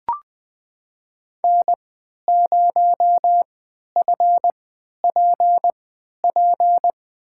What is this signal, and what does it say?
Morse code, heard as a keyed tone, about 20 wpm, spaced as for 11 wpm, 700 Hz. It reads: N0FPP
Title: USA Call Signs - 5 Characters - 3x Character Spacing - 20wpm